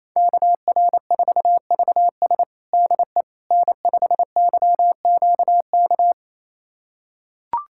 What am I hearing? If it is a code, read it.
Morse code, heard as a keyed tone, about 28 wpm, 700 Hz. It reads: KR4VS DE N5YQK
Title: USA Callsign DE Callsign - 5 Characters - 28wpm